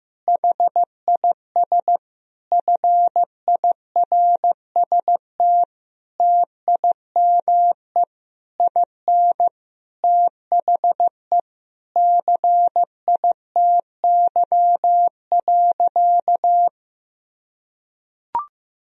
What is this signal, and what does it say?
Morse code, heard as a keyed tone, about 15 wpm, 700 Hz. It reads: HIS FIRST TIME IN THE CITY.
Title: Sentences from Top 1000 Words - Review Part 4 - 15wpm